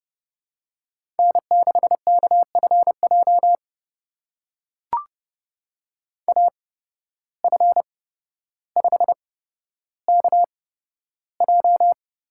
Morse code, heard as a keyed tone, about 30 wpm, spaced as for 13 wpm, 700 Hz. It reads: N6KFJ AF5KJ
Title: USA Call Signs - 5 Characters - 8x Character Spacing - 30wpm